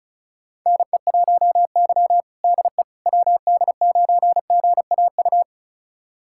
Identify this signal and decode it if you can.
Morse code, heard as a keyed tone, about 36 wpm, 700 Hz. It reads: NE1Y DE WD9GAU
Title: USA Callsign DE Callsign 35wpm